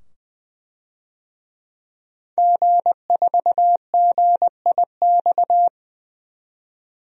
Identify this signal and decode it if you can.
Morse code, heard as a keyed tone, about 21 wpm, 700 Hz. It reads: G4GIX
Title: DXCC-100 Call Signs - 5 Characters - 20wpm